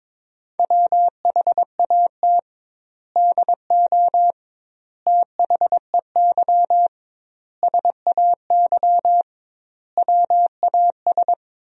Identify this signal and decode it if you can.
Morse code, heard as a keyed tone, about 22 wpm, 700 Hz. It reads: WHAT DO THEY SAY WAS
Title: 2x Word-Spacing - Sentences from Top 200 Words - 22wpm